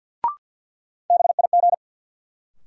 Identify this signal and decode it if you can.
Morse code, heard as a keyed tone, about 48 wpm, 700 Hz. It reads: BIG
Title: Sets of 2 Words - New Word Drill - 50wpm